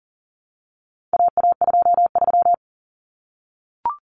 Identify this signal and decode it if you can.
Morse code, heard as a keyed tone, about 40 wpm, 700 Hz. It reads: AA23
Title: Common Cut Numbers - 4-Digit 40wpm